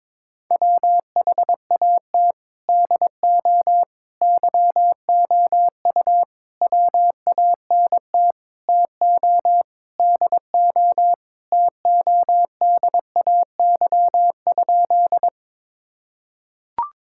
Morse code, heard as a keyed tone, about 22 wpm, 700 Hz. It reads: WHAT DO YOU WANT TO DO TODAY?
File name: practice / 22wpm / Sentences with 4-Word Phrases - 22wpm